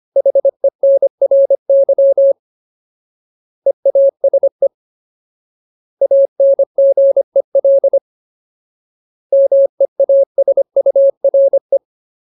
Morse code, heard as a keyed tone, about 25 wpm, 550 Hz. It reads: HENRY EASE ANGEL MEASURE
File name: Ultimate Word List / 25WPM Ultimate